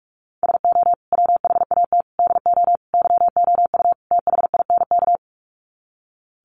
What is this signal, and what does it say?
Morse code, heard as a keyed tone, about 44 wpm, 700 Hz. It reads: SO WHAT DO YOU THINK
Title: Sets of 5 Words Encore2 45wpm